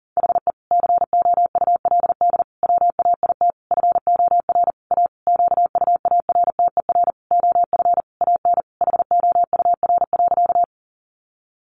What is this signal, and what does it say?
Morse code, heard as a keyed tone, about 40 wpm, 700 Hz. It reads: HE COULD WAIT FOR A QUARTER OF AN HOUR.
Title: Sentences from Top 1300 Words 40wpm